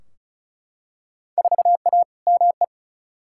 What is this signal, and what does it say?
Morse code, heard as a keyed tone, about 34 wpm, 700 Hz. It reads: 4A ME
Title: Field Day Exchanges - 35wpm